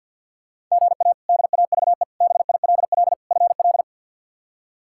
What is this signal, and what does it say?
Morse code, heard as a keyed tone, about 53 wpm, 700 Hz. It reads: GA DAVE BILL FL
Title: K1USN SST QSO Practice - 50wpm